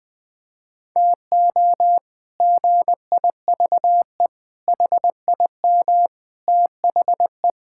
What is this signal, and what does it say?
Morse code, heard as a keyed tone, about 21 wpm, 700 Hz. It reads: TO GIVE HIM THE
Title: Sets of 8 Words - 20wpm